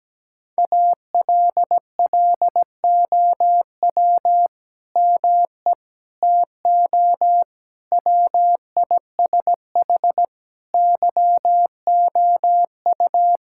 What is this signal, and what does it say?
Morse code, heard as a keyed tone, about 17 wpm, 700 Hz. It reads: ALLOW ME TO WISH YOU
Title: Sentences from Top 1600 Words 17wpm